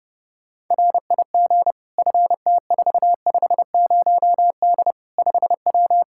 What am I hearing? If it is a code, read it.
Morse code, heard as a keyed tone, about 32 wpm, 700 Hz. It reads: RIG FT450D 5W
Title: Practice QSO - Exchange 2 - Info - 30wpm